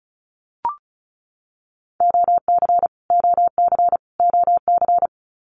Morse code, heard as a keyed tone, about 34 wpm, 700 Hz. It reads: OC OC OC
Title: Abbreviations Repeated 3x - 35wpm